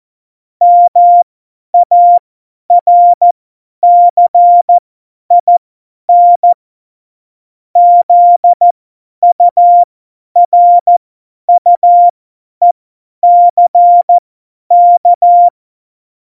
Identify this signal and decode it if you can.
Morse code, heard as a keyed tone, about 14 wpm, spaced as for 10 wpm, 700 Hz. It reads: MARCIN ZURUECK